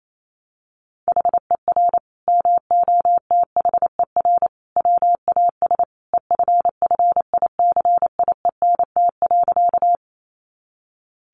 Morse code, heard as a keyed tone, about 28 wpm, 700 Hz. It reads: HER MOTHER WAS EFFICIENT.
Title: Sentences from Top 1300 Words 28wpm